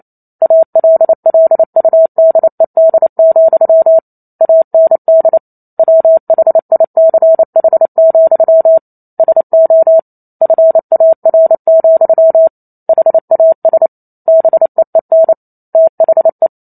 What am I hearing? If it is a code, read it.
Morse code, heard as a keyed tone, about 28 wpm, 650 Hz. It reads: ALLUDED, AND WHICH, SO FAR, HAS BEEN THE